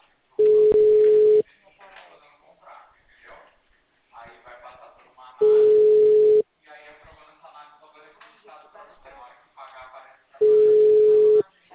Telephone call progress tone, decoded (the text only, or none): ringback tone